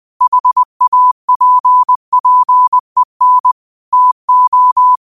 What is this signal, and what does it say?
Morse code, heard as a keyed tone, about 20 wpm, 1000 Hz. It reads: HAPPEN TO
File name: qotdfiles / 20wpm